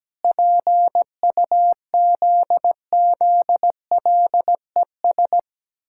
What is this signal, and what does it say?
Morse code, heard as a keyed tone, about 17 wpm, 700 Hz. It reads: PUZZLES
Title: Pangram Challenge v2 - 17wpm